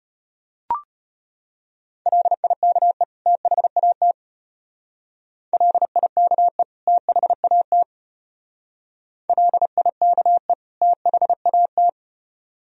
Morse code, heard as a keyed tone, about 34 wpm, 700 Hz. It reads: LIKE THAT LIKE THAT LIKE THAT